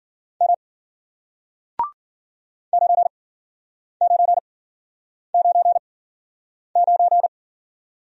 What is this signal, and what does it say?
Morse code, heard as a keyed tone, about 51 wpm, spaced as for 9 wpm, 700 Hz. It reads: M 9999